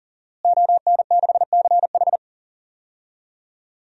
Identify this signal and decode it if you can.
Morse code, heard as a keyed tone, about 40 wpm, 700 Hz. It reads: ON6CH
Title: DXCC-100 Call Signs - 5 Characters - Repeated 3x - 40wpm